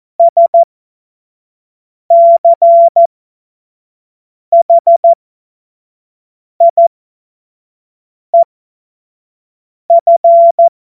Morse code, heard as a keyed tone, about 14 wpm, spaced as for 5 wpm, 680 Hz. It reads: SCHIEF
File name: Morsekurs / Witze / Witz 07.November 2023 25 F70